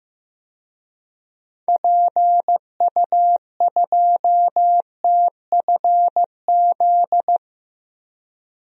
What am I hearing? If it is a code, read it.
Morse code, heard as a keyed tone, about 15 wpm, 700 Hz. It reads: PU2TFZ
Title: DXCC-100 Call Signs - 6 Characters - Repeated 2x - 15wpm